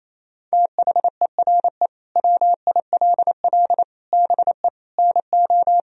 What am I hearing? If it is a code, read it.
Morse code, heard as a keyed tone, about 28 wpm, 700 Hz. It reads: THERE WILL BE NO
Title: Sentences from Top 1000 Words - Review Part 4 - 28wpm